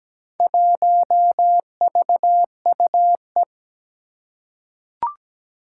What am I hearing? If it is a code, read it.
Morse code, heard as a keyed tone, about 17 wpm, 700 Hz. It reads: 1VUE
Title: Four Challenging Letter-Number Combinations 17wpm